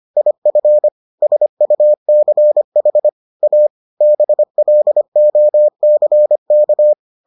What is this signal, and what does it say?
Morse code, heard as a keyed tone, about 25 wpm, 600 Hz. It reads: IF SUCH A BLOCK